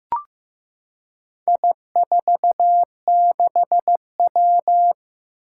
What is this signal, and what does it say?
Morse code, heard as a keyed tone, about 16 wpm, 700 Hz. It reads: I46W
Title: Four Challenging Letter-Number Combinations 15wpm